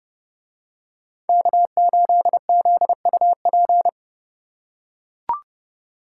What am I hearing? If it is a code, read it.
Morse code, heard as a keyed tone, about 28 wpm, 700 Hz. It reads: K8ZUP